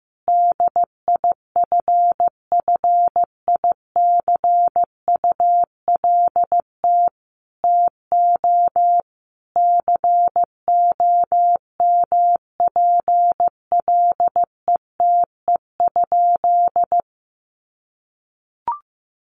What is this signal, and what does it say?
Morse code, heard as a keyed tone, about 14 wpm, 700 Hz. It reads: DIFFICULT TO COMPLETE?
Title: Sentences from Top 1500 Words 15wpm